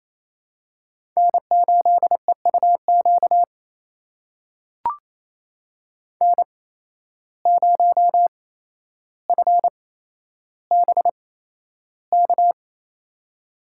Morse code, heard as a keyed tone, about 29 wpm, spaced as for 12 wpm, 700 Hz. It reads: N8EUQ N0FBK
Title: USA Call Signs - 5 Characters - 8x Character Spacing - 28wpm